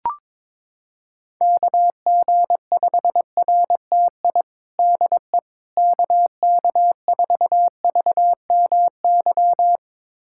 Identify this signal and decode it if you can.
Morse code, heard as a keyed tone, about 21 wpm, 700 Hz. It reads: KG5RTI DE KK4VMY
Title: USA Callsign DE Callsign 22wpm